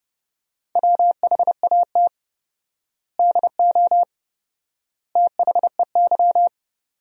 Morse code, heard as a keyed tone, about 32 wpm, 700 Hz. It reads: WHAT DO THEY
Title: 4x Word-Spacing - Sentences from Top 100 Words - 30wpm